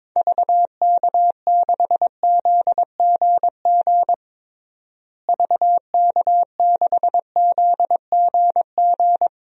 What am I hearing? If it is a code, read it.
Morse code, heard as a keyed tone, about 22 wpm, 700 Hz. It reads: VK6ZGG VK6ZGG
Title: DXCC-100 Call Signs - 6 Characters - Repeated 2x - 22wpm